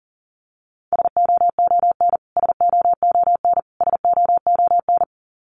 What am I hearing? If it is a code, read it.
Morse code, heard as a keyed tone, about 40 wpm, 700 Hz. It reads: SOON SOON SOON
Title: Top 500 Words Repeated 40wpm